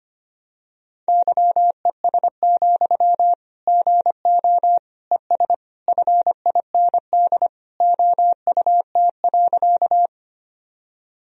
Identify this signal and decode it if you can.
Morse code, heard as a keyed tone, about 25 wpm, 700 Hz. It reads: YES, GO ES FIND OUT.
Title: Sentences from Top 500 Words and Abbreviations 25wpm